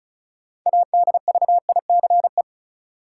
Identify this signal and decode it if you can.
Morse code, heard as a keyed tone, about 36 wpm, 700 Hz. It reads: ADVICE
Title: Sentences from Top 1100 Words - New Word Drill - 35wpm